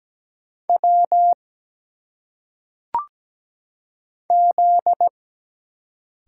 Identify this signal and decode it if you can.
Morse code, heard as a keyed tone, about 17 wpm, 700 Hz. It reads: W Z
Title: Single Letters - Phonetic Pronunciation 17wpm